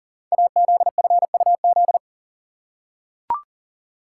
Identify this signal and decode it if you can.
Morse code, heard as a keyed tone, about 41 wpm, 700 Hz. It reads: AZFUZ
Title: Code Groups - 5 Letters - 40wpm